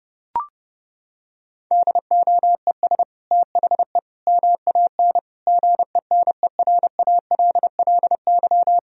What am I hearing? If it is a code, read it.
Morse code, heard as a keyed tone, about 30 wpm, 700 Hz. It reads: DOES THE MAN GENERALLY
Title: Sentences from Top 1000 Words - Review Part 4 - 30wpm